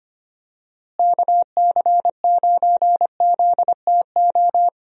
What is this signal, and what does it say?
Morse code, heard as a keyed tone, about 25 wpm, 700 Hz. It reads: KC9ZTO
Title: USA Call Signs - 6 Characters - Repeated 3x - 25wpm